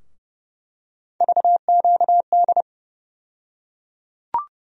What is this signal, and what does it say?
Morse code, heard as a keyed tone, about 31 wpm, 700 Hz. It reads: VQD
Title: USA Call Sign Postfixes - 3 Letters 30wpm